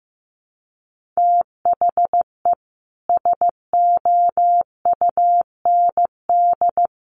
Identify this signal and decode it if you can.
Morse code, heard as a keyed tone, about 15 wpm, 700 Hz. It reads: THE SOUND
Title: Sets of 3 Words 15wpm